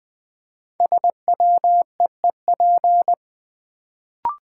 Morse code, heard as a keyed tone, about 20 wpm, 700 Hz. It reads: SWEEP
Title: Build A Word - 1 Letter At A Time - 5-Letter Words - 20wpm